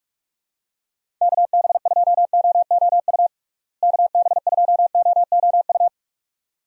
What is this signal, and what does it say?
Morse code, heard as a keyed tone, about 41 wpm, 700 Hz. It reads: KB2OOU KB2OOU
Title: USA Call Signs - 6 Characters - Repeated 2x - 45wpm